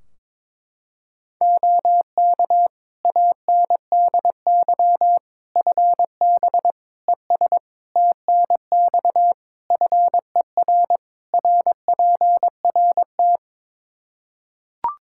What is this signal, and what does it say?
Morse code, heard as a keyed tone, about 22 wpm, 700 Hz. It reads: OK ANDY FB ES TNX FER RPRT
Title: K1USN SST QSO Practice 22wpm